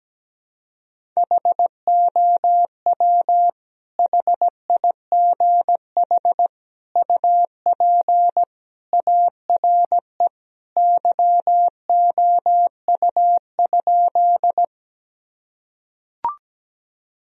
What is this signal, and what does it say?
Morse code, heard as a keyed tone, about 17 wpm, 700 Hz. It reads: HOW HIGH UP ARE YOU?
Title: Sentences from Top 1000 Words - Review Part 1 - 17wpm